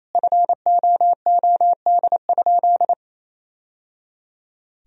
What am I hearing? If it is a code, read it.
Morse code, heard as a keyed tone, about 28 wpm, 700 Hz. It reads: FOOD?
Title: Sentences from Top 1000 Words - Review Part 1 - 28wpm